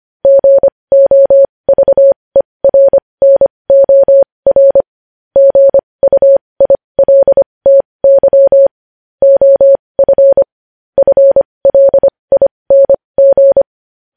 Morse code, heard as a keyed tone, about 25 wpm, spaced as for 20 wpm, 550 Hz. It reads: GOVERNOR GUILTY OF FLING
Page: news headlines in morse code at 20 wpm